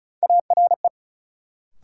Morse code, heard as a keyed tone, about 36 wpm, 700 Hz. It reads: ARE
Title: Sets of 2 Words - New Word Drill - 35wpm